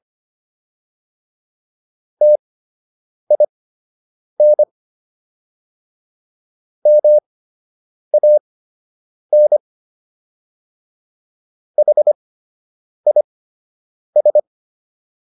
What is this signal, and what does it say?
Morse code, heard as a keyed tone, about 25 wpm, spaced as for 8 wpm, 600 Hz. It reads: TIN MAN HIS